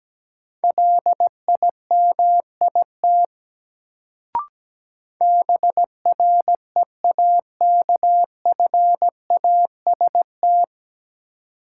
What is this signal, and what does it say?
Morse code, heard as a keyed tone, about 17 wpm, 700 Hz. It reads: LIMIT BREAKFAST
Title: Sentences from Top 1500 Words - New Word Drill - 17wpm